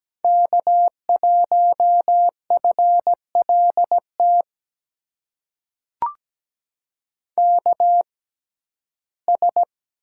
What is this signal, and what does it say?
Morse code, heard as a keyed tone, about 17 wpm, spaced as for 12 wpm, 700 Hz. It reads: K1FLT KS
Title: USA Call Signs - 5 Characters - 6x Character Spacing - 17wpm